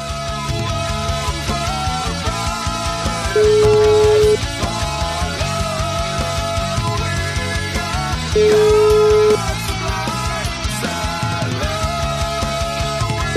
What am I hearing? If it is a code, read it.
Ringback tone (Europe, 425 Hz), with noise.